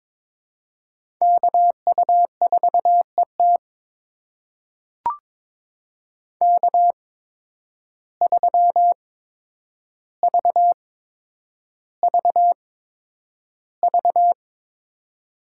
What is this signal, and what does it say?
Morse code, heard as a keyed tone, about 22 wpm, spaced as for 9 wpm, 700 Hz. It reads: KU4ET K3VVV
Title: USA Call Signs - 5 Characters - 8x Character Spacing - 22wpm